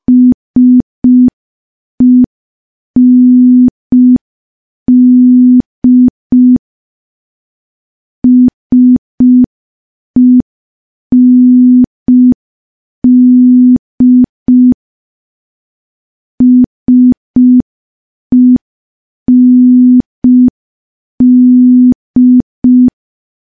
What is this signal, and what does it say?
Morse code, heard as a keyed tone, about 5 wpm, 250 Hz. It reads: SEND SEND SEND
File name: morse / morse-training-05WPM